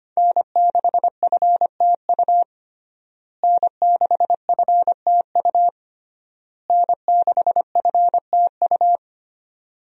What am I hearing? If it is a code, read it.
Morse code, heard as a keyed tone, about 25 wpm, 700 Hz. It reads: N6FTU N6FTU N6FTU
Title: USA Call Signs - 5 Characters - Repeated 3x - 25wpm